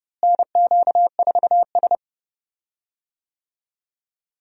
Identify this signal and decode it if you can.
Morse code, heard as a keyed tone, about 30 wpm, 700 Hz. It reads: NQ4S
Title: USA Call Signs - 4 Characters - Repeated 2x - 30wpm